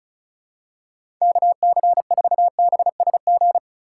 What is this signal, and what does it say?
Morse code, heard as a keyed tone, about 34 wpm, 700 Hz. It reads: KC4BSG
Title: USA Call Signs - 6 Characters - 3x Character Spacing - 35wpm